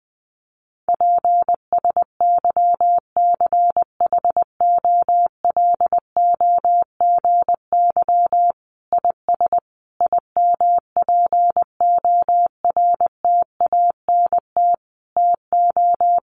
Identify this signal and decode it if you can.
Morse code, heard as a keyed tone, about 20 wpm, 700 Hz. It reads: PSYCHOLOGY IS IMPORTANT TO
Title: Sentences from Top 1400 Words 20wpm